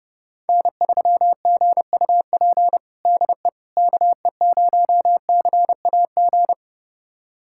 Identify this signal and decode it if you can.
Morse code, heard as a keyed tone, about 30 wpm, 700 Hz. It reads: N3GUP DE KE0CAG
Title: USA Callsign DE Callsign 30wpm